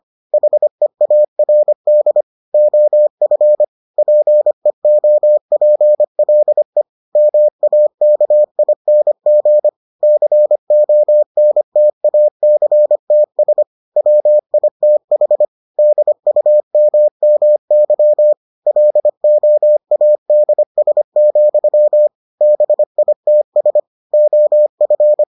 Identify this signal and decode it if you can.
Morse code, heard as a keyed tone, about 25 wpm, 600 Hz. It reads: HEARD OF PEOPLE MAKING CONTACTS WITH DUMMY LOADS, BITS OF